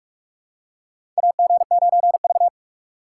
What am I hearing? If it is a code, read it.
Morse code, heard as a keyed tone, about 44 wpm, 700 Hz. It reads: AG9V
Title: USA Call Signs - 4 Characters - 3x Character Spacing - 45wpm